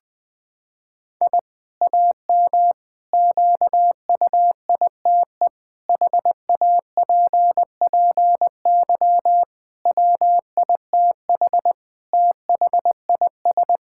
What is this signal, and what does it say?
Morse code, heard as a keyed tone, about 20 wpm, 700 Hz. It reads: I AM QUITE HAPPY WITH THIS
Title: Sentences from Top 1000 Words - Review Part 5 - 20wpm